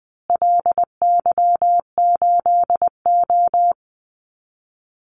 Morse code, heard as a keyed tone, about 20 wpm, 700 Hz. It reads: LY8O
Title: Call Signs Repeated 2x - 20wpm